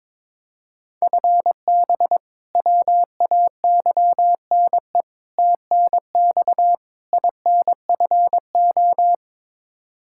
Morse code, heard as a keyed tone, about 22 wpm, 700 Hz. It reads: FB WAYNE TNX INFO
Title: K1USN SST QSO Practice 22wpm